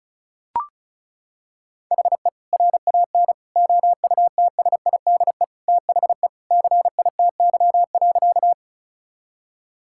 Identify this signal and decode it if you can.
Morse code, heard as a keyed tone, about 36 wpm, 700 Hz. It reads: HE RAN OUTSIDE THE CITY.
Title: Sentences from Top 1000 Words - Review Part 6 - 35wpm